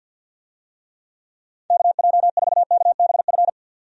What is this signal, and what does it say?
Morse code, heard as a keyed tone, about 51 wpm, 700 Hz. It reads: KJ4KBF
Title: USA Call Signs - 6 Characters - 50wpm